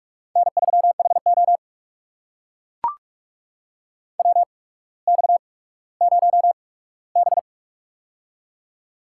Morse code, heard as a keyed tone, about 44 wpm, spaced as for 19 wpm, 700 Hz. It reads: N3HO WX0B
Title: USA Call Signs - 4 Characters - 8x Character Spacing - 45wpm